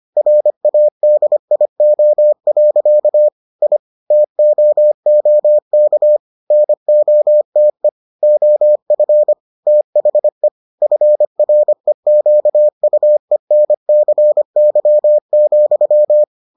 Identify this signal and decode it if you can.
Morse code, heard as a keyed tone, about 25 wpm, 600 Hz. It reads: RADIO. I TOOK NOTE OF THE FREQUENCY,